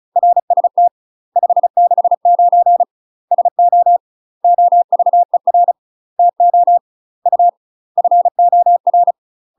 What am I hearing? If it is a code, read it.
Morse code, heard as a keyed tone, about 34 wpm, 700 Hz. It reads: RST 569 SO OVER TO U FOR
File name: UK MASTER QSOs TRAINING/35WPM